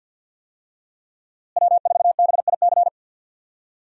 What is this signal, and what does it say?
Morse code, heard as a keyed tone, about 52 wpm, 700 Hz. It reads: W4BIC
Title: USA Call Signs - 5 Characters - Repeated 3x - 50wpm